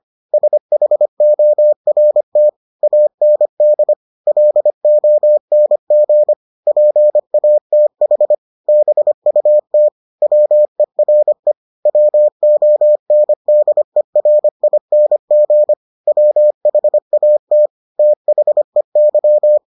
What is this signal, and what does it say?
Morse code, heard as a keyed tone, about 25 wpm, 600 Hz. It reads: SHORT AND LONG PATH BUT WERE WONDERING WHAT THEY